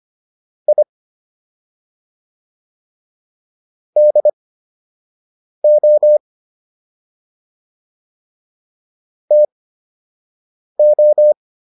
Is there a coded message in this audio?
Yes: Morse code from 0.7 to 11.3 s, heard as a keyed tone, about 25 wpm, spaced as for 6 wpm, 600 Hz. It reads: I DO TO